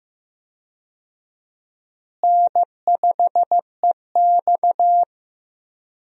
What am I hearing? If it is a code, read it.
Morse code, heard as a keyed tone, about 15 wpm, 700 Hz. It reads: N5EX
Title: USA Call Signs - 4 Characters - Repeated 2x - 15wpm